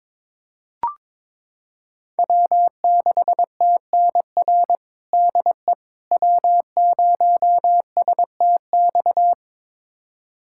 Morse code, heard as a keyed tone, about 22 wpm, 700 Hz. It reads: W6TNR DE W0STX
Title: USA Callsign DE Callsign - 5 Characters - 22wpm